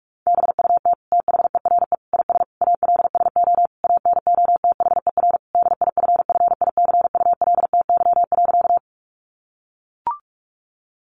Morse code, heard as a keyed tone, about 44 wpm, 700 Hz. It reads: BUT THERE IS ALSO ANOTHER DIFFICULTY.
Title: Sentences from Top 1500 Words 45wpm